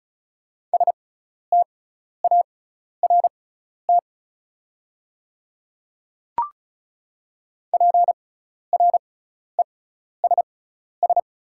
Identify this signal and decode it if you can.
Morse code, heard as a keyed tone, about 36 wpm, spaced as for 12 wpm, 690 Hz. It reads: START PRESS
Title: 6x Character Spacing - Top 300 Words - 35wpm